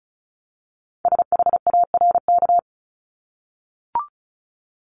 Morse code, heard as a keyed tone, about 35 wpm, 700 Hz. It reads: SHARK
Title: Multi-letter Phonemes 35wpm